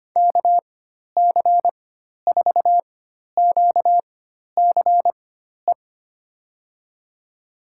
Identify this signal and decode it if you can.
Morse code, heard as a keyed tone, about 25 wpm, spaced as for 12 wpm, 700 Hz. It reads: KC4QCE
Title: USA Call Signs - 6 Characters - 4x Character Spacing - 25wpm